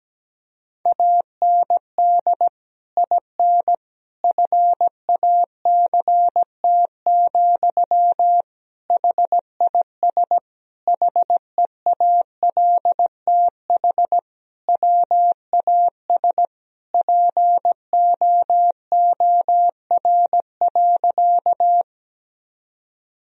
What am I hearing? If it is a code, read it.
Morse code, heard as a keyed tone, about 17 wpm, 700 Hz. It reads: AND IN FACT, HIS HEALTH WAS POOR.
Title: Sentences from Top 1000 Words - Review Part 3 - 17wpm